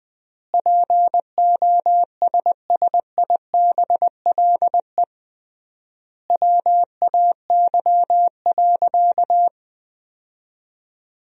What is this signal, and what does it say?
Morse code, heard as a keyed tone, about 20 wpm, 700 Hz. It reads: POSSIBLE WAY.
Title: 3x Word-Spacing - Sentences from Top 500 Words - 20wpm